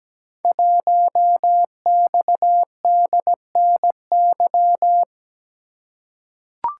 Morse code, heard as a keyed tone, about 17 wpm, 700 Hz. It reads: 1XDNY
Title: Code Groups - 5 Letter:Numbers - 17wpm